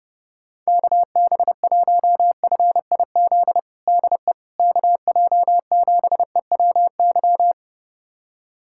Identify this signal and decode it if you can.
Morse code, heard as a keyed tone, about 32 wpm, 700 Hz. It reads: KB1FIZ DE KJ7EWY
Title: USA Callsign DE Callsign 30wpm